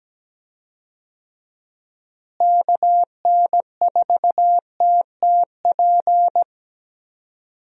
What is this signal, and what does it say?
Morse code, heard as a keyed tone, about 17 wpm, 700 Hz. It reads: KN4TTP
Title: USA Call Signs - 6 Characters - Repeated 2x - 17wpm